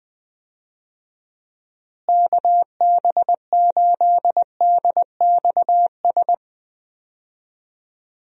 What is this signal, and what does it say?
Morse code, heard as a keyed tone, about 20 wpm, 700 Hz. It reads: KB8DXS
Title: USA Call Signs - 6 Characters - Repeated 2x - 20wpm